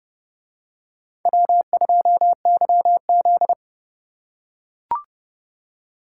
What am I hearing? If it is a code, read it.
Morse code, heard as a keyed tone, about 28 wpm, 700 Hz. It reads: W2YZ